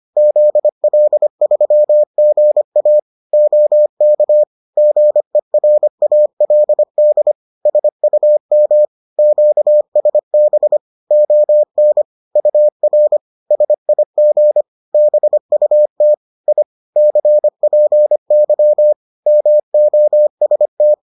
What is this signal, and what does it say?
Morse code, heard as a keyed tone, about 25 wpm, 600 Hz. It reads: ZL3GA OK GERALD SUM QSB ON UR SIG BUT I CPY MOST